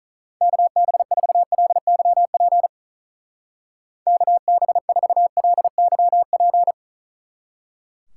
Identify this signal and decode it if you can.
Morse code, heard as a keyed tone, about 45 wpm, 700 Hz. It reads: KB4LYP KB4LYP